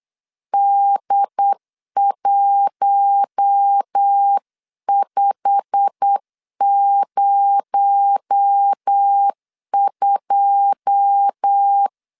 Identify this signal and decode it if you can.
Morse code, heard as a keyed tone, about 8 wpm, 800 Hz. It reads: D1502
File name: LEZIONI / 9 2 8wpm 3437